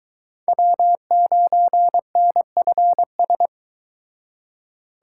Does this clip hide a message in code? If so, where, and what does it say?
Yes: Morse code from 0.5 to 3.5 s, heard as a keyed tone, about 22 wpm, 700 Hz. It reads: W9NFS